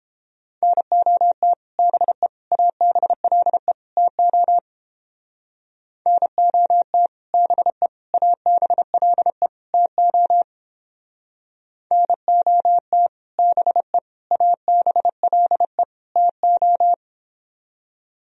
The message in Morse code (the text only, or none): NOT BE ABLE TO NOT BE ABLE TO NOT BE ABLE TO